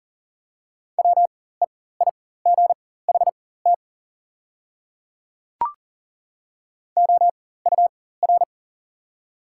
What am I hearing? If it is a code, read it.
Morse code, heard as a keyed tone, about 40 wpm, spaced as for 19 wpm, 690 Hz. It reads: WEIGHT OUR
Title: 4x Character Spacing - Top 500 Words - 40wpm